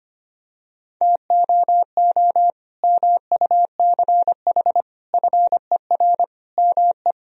Morse code, heard as a keyed tone, about 25 wpm, 700 Hz. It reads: TOO MUCH FER ME
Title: Sentences from Top 200 Words and Abbreviations 25wpm